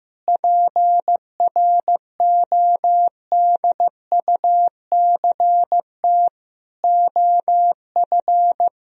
Morse code, heard as a keyed tone, about 16 wpm, 700 Hz. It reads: PRODUCT OF
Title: Sentences from Top 1000 Words - Review Part 2 - 15wpm